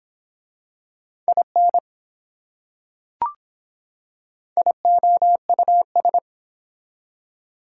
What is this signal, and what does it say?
Morse code, heard as a keyed tone, about 25 wpm, 700 Hz. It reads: IN IOUS